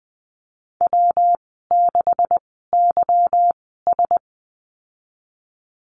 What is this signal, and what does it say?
Morse code, heard as a keyed tone, about 20 wpm, spaced as for 14 wpm, 700 Hz. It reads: W6YS
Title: 161 Lesson 26g - Letter g - Call Signs 20wpm